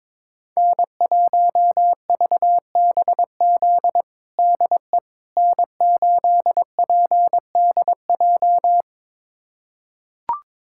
Morse code, heard as a keyed tone, about 22 wpm, 700 Hz. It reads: N1VBZ DE N8PDJ
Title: USA Callsign DE Callsign - 5 Characters - 22wpm